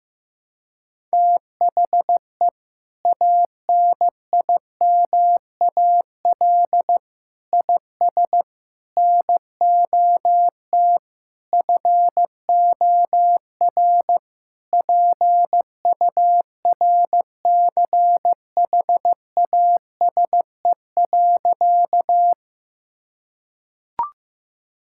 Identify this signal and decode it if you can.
Morse code, heard as a keyed tone, about 15 wpm, 700 Hz. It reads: THE ANIMAL IS NOT FOR PURCHASE.
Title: Sentences from Top 1000 Words - Review Part 5 - 15wpm